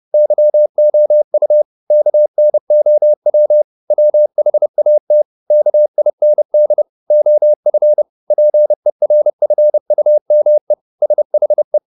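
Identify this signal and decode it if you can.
Morse code, heard as a keyed tone, about 30 wpm, 600 Hz. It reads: YOU KNOW WHAT KIND OF PERFUME SHE
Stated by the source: Morse code at 30 wpm, the novel Never by Ken Follett in English, from CW med Gnister.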